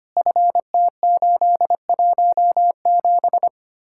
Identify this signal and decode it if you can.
Morse code, heard as a keyed tone, about 31 wpm, 700 Hz. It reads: FT817
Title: Speed-Racing Common QSO Phrases Encore - 25wpm